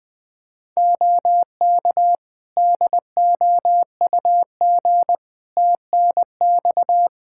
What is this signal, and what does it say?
Morse code, heard as a keyed tone, about 21 wpm, 700 Hz. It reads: OK DOUG TNX
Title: K1USN SST QSO Practice 20wpm